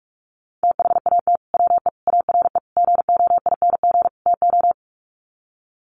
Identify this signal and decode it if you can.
Morse code, heard as a keyed tone, about 44 wpm, 700 Hz. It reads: THAT WE ARE GOING TO
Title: Sets of 5 Words Encore 45wpm